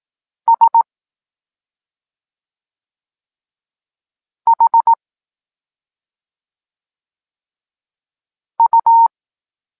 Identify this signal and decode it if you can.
Morse code, heard as a keyed tone, about 18 wpm, 920 Hz. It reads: S H U